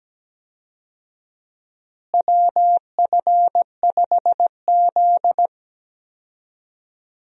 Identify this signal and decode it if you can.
Morse code, heard as a keyed tone, about 17 wpm, 700 Hz. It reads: WF5Z
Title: USA Call Signs - 4 Characters - Repeated 3x - 17wpm